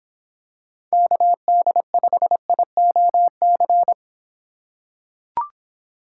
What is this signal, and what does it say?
Morse code, heard as a keyed tone, about 24 wpm, 700 Hz. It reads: KD5IOC